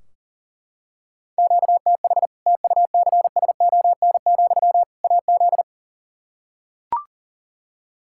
Morse code, heard as a keyed tone, about 40 wpm, 700 Hz. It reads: QTH TUCSON, AZ
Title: Practice QSO - Exchange 1 - Beginning - 40wpm